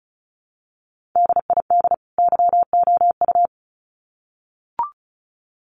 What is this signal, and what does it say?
Morse code, heard as a keyed tone, about 34 wpm, 700 Hz. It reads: DID YOU
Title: Sets of 2 Words 35wpm